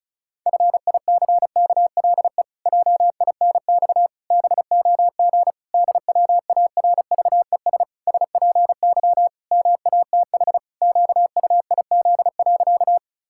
Morse code, heard as a keyed tone, about 36 wpm, 700 Hz. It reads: FICKLE JINX BOG DWARVES SPY MATH QUIZ.
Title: Pangram Challenge v2 - 35wpm